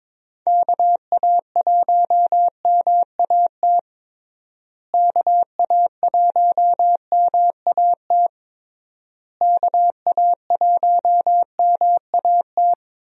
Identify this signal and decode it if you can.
Morse code, heard as a keyed tone, about 22 wpm, 700 Hz. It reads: KA1MAT KA1MAT KA1MAT
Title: USA Call Signs - 6 Characters - Repeated 3x - 22wpm